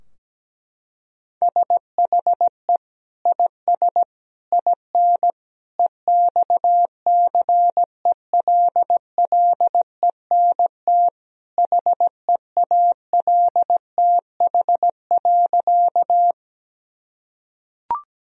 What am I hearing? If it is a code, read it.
Morse code, heard as a keyed tone, about 17 wpm, 700 Hz. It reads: SHE IS IN EXCELLENT HEALTH.
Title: Sentences from Top 1000 Words - Review Part 3 - 17wpm